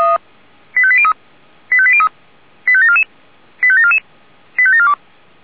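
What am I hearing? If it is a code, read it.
Five-tone selective call (ZVEI), heard as quick runs of five tones, tones of about 70 ms. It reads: 85803 85803 85744 85744 85742
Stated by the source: Five-tone ZVEI selective call received by SDR.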